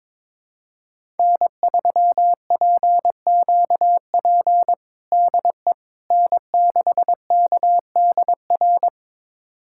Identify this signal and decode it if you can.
Morse code, heard as a keyed tone, about 22 wpm, 700 Hz. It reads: N3PQP DE N6KDR
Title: USA Callsign DE Callsign - 5 Characters - 22wpm